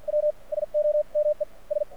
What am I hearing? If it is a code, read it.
Morse code, heard as a keyed tone, about 50 wpm, spaced as for 42 wpm, 600 Hz. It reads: W SOME S
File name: archived / CRSnet